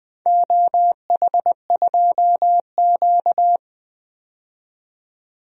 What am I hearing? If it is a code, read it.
Morse code, heard as a keyed tone, about 20 wpm, 700 Hz. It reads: OH2Q
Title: DXCC-100 Call Signs - 4 Characters - 20wpm